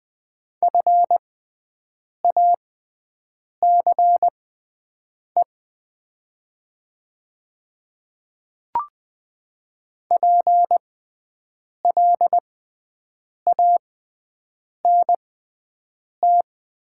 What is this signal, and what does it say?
Morse code, heard as a keyed tone, about 20 wpm, spaced as for 7 wpm, 700 Hz. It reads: FACE PLANT
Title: 6x Character Spacing - Top 300 Words - 20wpm